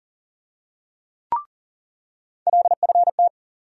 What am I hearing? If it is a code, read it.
Morse code, heard as a keyed tone, about 40 wpm, 700 Hz. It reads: LFT
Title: USA Call Sign Postfixes - 3 Letters 40wpm